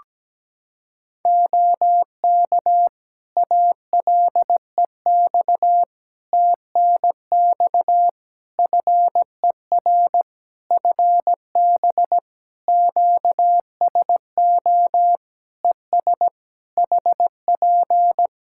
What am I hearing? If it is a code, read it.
Morse code, heard as a keyed tone, about 17 wpm, 700 Hz. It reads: OK ALEX TNX FER FB QSO ES HP
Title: Practice QSO - Exchange 4 - Ending - 17wpm